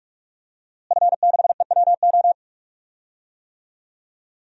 Morse code, heard as a keyed tone, about 41 wpm, 700 Hz. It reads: F6EWO
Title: DXCC-100 Call Signs - 5 Characters - Repeated 3x - 45wpm